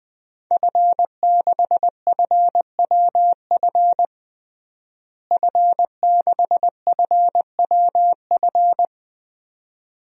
Morse code, heard as a keyed tone, about 20 wpm, 700 Hz. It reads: F6FWF F6FWF
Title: DXCC-100 Call Signs - 5 Characters - Repeated 2x - 20wpm